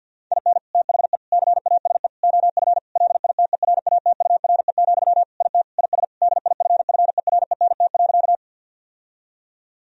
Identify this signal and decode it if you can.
Morse code, heard as a keyed tone, about 50 wpm, 700 Hz. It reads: IN THE CASE OF LITERATURE, IT IS DIFFERENT.
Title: Sentences from Top 1000 Words - Review Part 3 - 50wpm